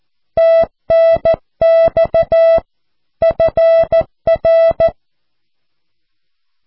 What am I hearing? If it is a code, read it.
Morse code, heard as a keyed tone, about 14 wpm, 660 Hz. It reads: TNX FR
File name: cw / QSO